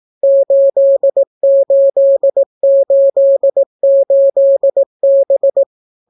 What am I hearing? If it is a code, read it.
Morse code, heard as a keyed tone, about 18 wpm, 550 Hz. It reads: 8888B